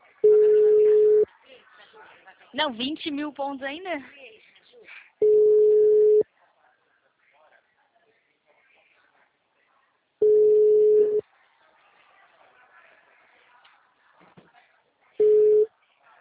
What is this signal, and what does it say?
Ringback tone (Europe, 425 Hz), with speech.